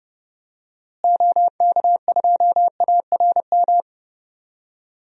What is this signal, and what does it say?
Morse code, heard as a keyed tone, about 30 wpm, 700 Hz. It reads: OK2ARM
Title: DXCC-100 Call Signs - 6 Characters - Repeated 3x - 30wpm